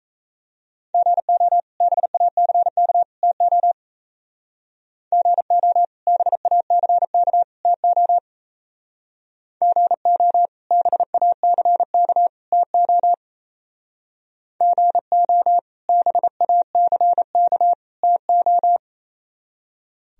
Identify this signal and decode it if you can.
Morse code, heard as a keyed tone, about 34 wpm, 700 Hz. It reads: GO BACK TO GO BACK TO GO BACK TO GO BACK TO
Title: Speed-Racing - Sets of 3 Words 28wpm